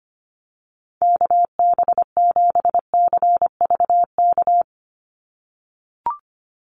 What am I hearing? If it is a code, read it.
Morse code, heard as a keyed tone, about 25 wpm, 700 Hz. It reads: KB7CVK
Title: USA Call Signs 25wpm